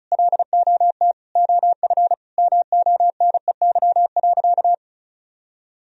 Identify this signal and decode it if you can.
Morse code, heard as a keyed tone, about 36 wpm, 700 Hz. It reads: LOT OF MONEY.
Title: Sentences from Top 1000 Words - Review Part 4 - 35wpm